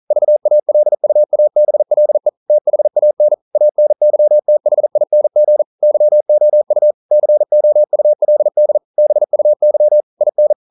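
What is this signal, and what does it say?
Morse code, heard as a keyed tone, about 42 wpm, 600 Hz. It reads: VALUABLE THAN ANYTHING YOU COULD BUY IN
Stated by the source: Morse code practice at 41 wpm, from CW med Gnister.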